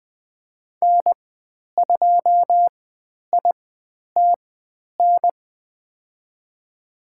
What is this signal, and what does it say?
Morse code, heard as a keyed tone, about 20 wpm, spaced as for 10 wpm, 700 Hz. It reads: N2ITN
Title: 102 Lesson 17g - Letter u - Call Signs 20wpm